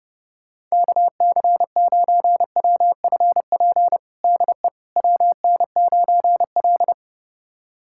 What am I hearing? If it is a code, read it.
Morse code, heard as a keyed tone, about 30 wpm, 700 Hz. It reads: KC9WFP DE WN9L
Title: USA Callsign DE Callsign 30wpm